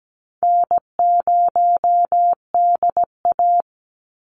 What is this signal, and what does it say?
Morse code, heard as a keyed tone, about 17 wpm, 700 Hz. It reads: N0DA
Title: NA SOTA Chaser Callsigns 17wpm